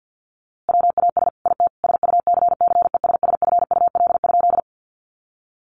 Morse code, heard as a keyed tone, about 50 wpm, 700 Hz. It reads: WAS IT SUCCESSFUL?
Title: A Sound Of Thunder 50wpm